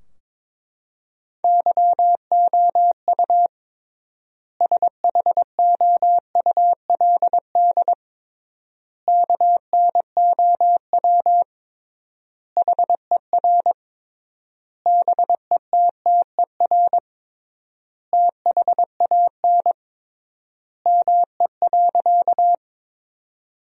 Morse code, heard as a keyed tone, about 22 wpm, 700 Hz. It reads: YOU SHOULD KNOW HER BETTER THAN ME.
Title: 3x Word-Spacing - Sentences from Top 400 Words - 22wpm